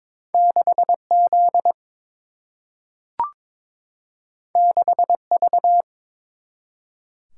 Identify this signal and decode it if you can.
Morse code, heard as a keyed tone, about 22 wpm, 700 Hz. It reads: 6Z 6V
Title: Single-Letter Single-Number 22wpm